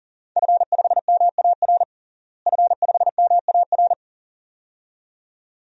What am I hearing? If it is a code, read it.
Morse code, heard as a keyed tone, about 40 wpm, 690 Hz. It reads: F5MAR F5MAR
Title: DXCC-100 Call Signs - 5 Characters - Repeated 2x - 40wpm